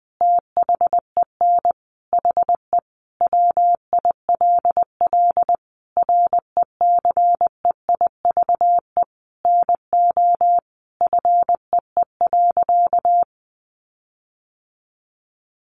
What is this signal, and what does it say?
Morse code, heard as a keyed tone, about 20 wpm, 700 Hz. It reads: THEN HE WILL RECEIVE NO FEE.
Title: Sentences from Top 1500 Words 20wpm